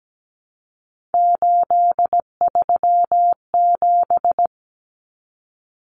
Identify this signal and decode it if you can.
Morse code, heard as a keyed tone, about 17 wpm, 700 Hz. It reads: 837
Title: Numbers - 3-Digit 17wpm